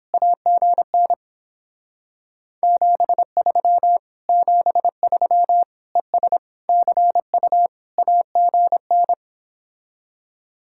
Morse code, heard as a keyed tone, about 29 wpm, 700 Hz. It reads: AGN 73 73 ES CU AGN